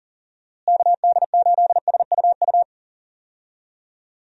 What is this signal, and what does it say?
Morse code, heard as a keyed tone, about 40 wpm, 700 Hz. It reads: KD8SUU
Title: USA Call Signs - 6 Characters - Repeated 2x - 40wpm